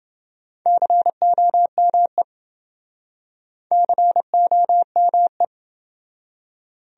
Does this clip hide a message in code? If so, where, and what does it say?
Yes: Morse code from 0.7 to 5.4 s, heard as a keyed tone, about 27 wpm, 700 Hz. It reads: COME COME